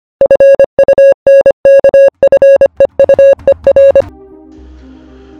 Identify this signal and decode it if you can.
Morse code, heard as a keyed tone, about 25 wpm, 550 Hz. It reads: FUNKFEUER
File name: funkfeuer / funkfeuer-files / SonderFunkfeuer-Advent